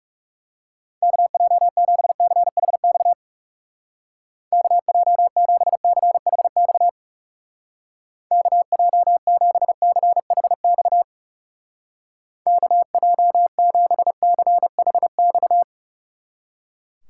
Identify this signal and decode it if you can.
Morse code, heard as a keyed tone, about 37 wpm, 700 Hz. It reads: KJ7CHX KJ7CHX KJ7CHX KJ7CHX
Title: Speed Racing - USA 2x3 Call Signs 30wpm